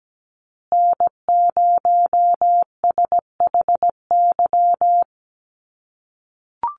Morse code, heard as a keyed tone, about 17 wpm, 700 Hz. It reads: N0SHY
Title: USA Call Signs Repeated 2x 17wpm